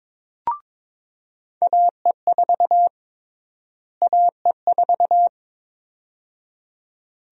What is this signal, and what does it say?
Morse code, heard as a keyed tone, about 22 wpm, 700 Hz. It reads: AE4 AE4
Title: USA Call Sign Prefixes - Repeated 2x - 22wpm